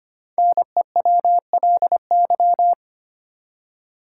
Morse code, heard as a keyed tone, about 25 wpm, 700 Hz. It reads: NEWLY
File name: 5 Letter Words - First 3 Letters Given - 25wpm